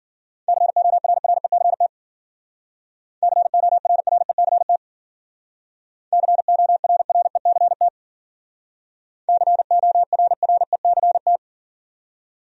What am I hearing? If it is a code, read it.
Morse code, heard as a keyed tone, about 53 wpm, 700 Hz. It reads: CORRECT CORRECT CORRECT CORRECT